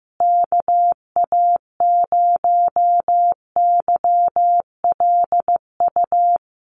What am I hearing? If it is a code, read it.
Morse code, heard as a keyed tone, about 15 wpm, 700 Hz. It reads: KA0YLU
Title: USA Call Signs Repeated 2x 15wpm